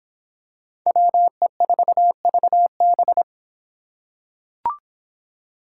Morse code, heard as a keyed tone, about 26 wpm, 700 Hz. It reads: WE4VB